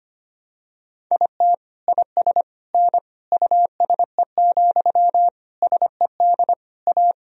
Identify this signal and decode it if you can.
Morse code, heard as a keyed tone, about 25 wpm, 700 Hz. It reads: IT IS N USE, SED A
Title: Sentences from Top 500 Words and Abbreviations 25wpm